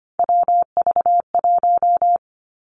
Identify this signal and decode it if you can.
Morse code, heard as a keyed tone, about 25 wpm, 700 Hz. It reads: WV1
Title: Three Challenging Letter-Number Combinations 25wpm